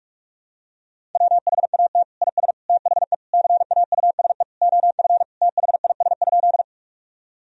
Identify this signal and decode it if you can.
Morse code, heard as a keyed tone, about 46 wpm, 700 Hz. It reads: WHAT IS THE CAUSE OF THIS?
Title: Sentences from Top 1000 Words - Review Part 1 - 45wpm